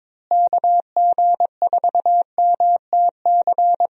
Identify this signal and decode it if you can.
Morse code, heard as a keyed tone, about 21 wpm, 700 Hz. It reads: KG4MTC
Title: USA Call Signs - 6 Characters - Repeated 3x - 22wpm